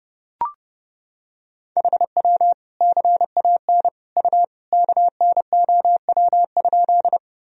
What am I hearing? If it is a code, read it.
Morse code, heard as a keyed tone, about 30 wpm, 700 Hz. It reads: HW CAN U KNOW?
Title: Sentences from Top 100 Words and Abbreviations 30wpm